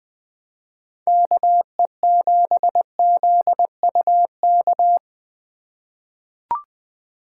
Morse code, heard as a keyed tone, about 20 wpm, 700 Hz. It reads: KE7ZUK
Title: USA Call Signs - 6 Characters - 4x Character Spacing - 20wpm